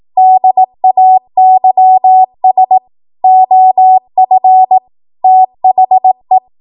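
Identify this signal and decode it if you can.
Morse code, heard as a keyed tone, about 18 wpm, 750 Hz. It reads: DAYS OF THE
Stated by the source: ARRL 18 wpm Morse code practice archive transmission.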